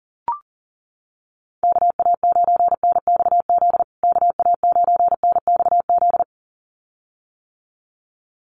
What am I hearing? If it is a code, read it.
Morse code, heard as a keyed tone, about 40 wpm, 700 Hz. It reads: KA9NXZ KA9NXZ
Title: USA Call Signs Repeated 2x 40wpm